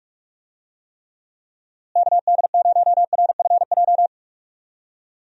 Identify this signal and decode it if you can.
Morse code, heard as a keyed tone, about 42 wpm, 700 Hz. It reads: KD0RFJ
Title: USA Call Signs - 6 Characters - Repeated 3x - 45wpm